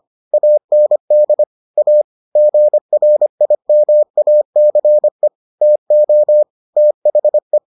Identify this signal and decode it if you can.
Morse code, heard as a keyed tone, about 25 wpm, 600 Hz. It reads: AND A GRIMACE TO THE